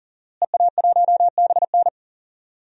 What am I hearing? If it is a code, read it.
Morse code, heard as a keyed tone, about 37 wpm, 700 Hz. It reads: EA1BN
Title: DXCC-100 Call Signs - 5 Characters - Repeated 2x - 40wpm